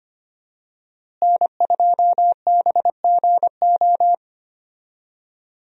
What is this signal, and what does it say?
Morse code, heard as a keyed tone, about 25 wpm, 700 Hz. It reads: N2BGO
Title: USA Call Signs - 5 Characters - Repeated 3x - 25wpm